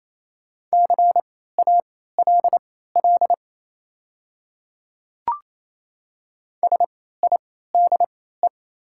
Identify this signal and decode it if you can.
Morse code, heard as a keyed tone, about 27 wpm, spaced as for 16 wpm, 700 Hz. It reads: CALL SIDE
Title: 3x Character Spacing - Top 100 Words - 28wpm